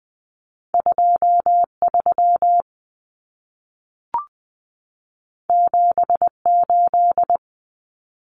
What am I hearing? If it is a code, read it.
Morse code, heard as a keyed tone, about 20 wpm, 700 Hz. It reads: 23 78
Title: Common Cut Numbers - 2-Digit 20wpm